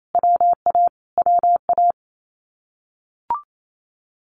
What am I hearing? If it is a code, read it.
Morse code, heard as a keyed tone, about 28 wpm, 700 Hz. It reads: WA WA
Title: Common QSO Phrases Encore 28wpm